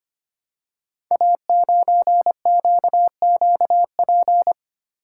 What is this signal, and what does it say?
Morse code, heard as a keyed tone, about 25 wpm, 700 Hz. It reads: A9QQP
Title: Code Groups - 5 Letter:Numbers - 25wpm